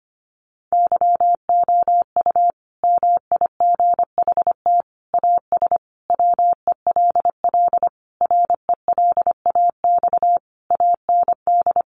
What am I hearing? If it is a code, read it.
Morse code, heard as a keyed tone, about 25 wpm, 700 Hz. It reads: YOU MIGHT AS WELL RELAX AND
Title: Sentences from Top 1400 Words 25wpm